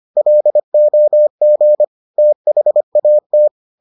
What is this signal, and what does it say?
Morse code, heard as a keyed tone, about 25 wpm, 600 Hz. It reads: LOG THAT